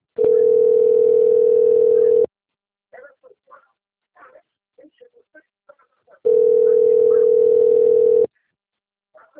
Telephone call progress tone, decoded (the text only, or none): ringback tone